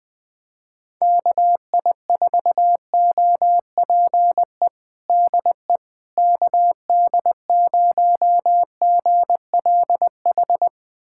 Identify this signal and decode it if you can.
Morse code, heard as a keyed tone, about 21 wpm, 700 Hz. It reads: KI4OPE DE KD0GLH
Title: USA Callsign DE Callsign 20wpm